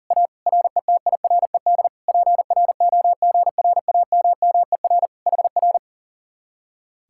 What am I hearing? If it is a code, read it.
Morse code, heard as a keyed tone, about 51 wpm, 700 Hz. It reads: A RETIRED PROGRAMMER HR